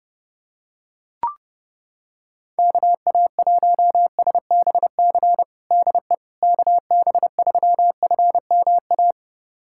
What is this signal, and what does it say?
Morse code, heard as a keyed tone, about 31 wpm, 700 Hz. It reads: KA1SBC DE KB3FMA
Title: USA Callsign DE Callsign - 6 Characters - 30wpm